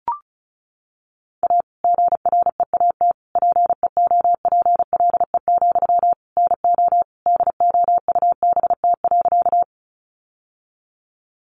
Morse code, heard as a keyed tone, about 34 wpm, 700 Hz. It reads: A GREAT PEOPLE, NO DOUBT.
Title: Sentences from Top 1500 Words 35wpm